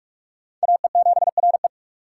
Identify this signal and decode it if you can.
Morse code, heard as a keyed tone, about 48 wpm, 700 Hz. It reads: AE7RE